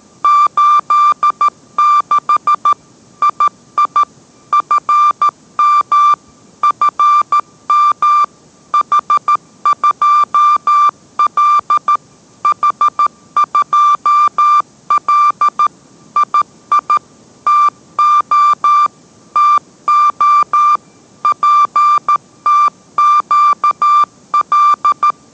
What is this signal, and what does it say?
Morse code, heard as a keyed tone, about 13 wpm, 1180 Hz. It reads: 86 II FM FM H2L H2L II TO TO PTQL